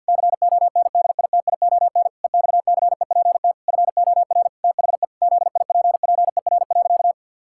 Morse code, heard as a keyed tone, about 52 wpm, 700 Hz. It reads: CONDITION EXCEPT FOR THE ZIPPER.
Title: Pangram Challenge v2 - 50wpm